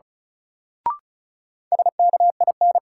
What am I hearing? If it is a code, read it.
Morse code, heard as a keyed tone, about 35 wpm, 700 Hz. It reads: SKIN
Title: Sentences from Top 1300 Words - New Word Drill Alt - 35wpm